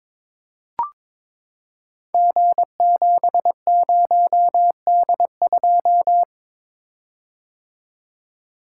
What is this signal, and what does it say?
Morse code, heard as a keyed tone, about 22 wpm, 700 Hz. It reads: G70D2
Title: Code Groups - 5 Letter:Numbers - 22wpm